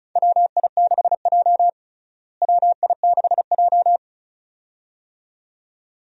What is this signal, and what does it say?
Morse code, heard as a keyed tone, about 36 wpm, 700 Hz. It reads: WI6J WI6J
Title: USA Call Signs - 4 Characters - Repeated 2x - 35wpm